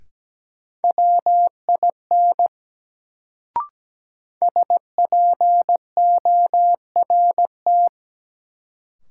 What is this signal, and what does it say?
Morse code, heard as a keyed tone, about 17 wpm, 700 Hz. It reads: WIN SPORT
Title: Sentences from Top 1500 Words - New Word Drill Alt - 17wpm